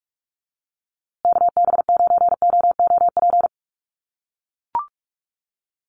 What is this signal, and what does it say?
Morse code, heard as a keyed tone, about 44 wpm, 700 Hz. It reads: KB9OOP
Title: USA Call Signs Repeated 3x 45wpm